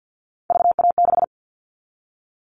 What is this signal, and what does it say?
Morse code, heard as a keyed tone, about 50 wpm, 700 Hz. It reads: VA6
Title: All Cut Numbers - 3-Digit 50wpm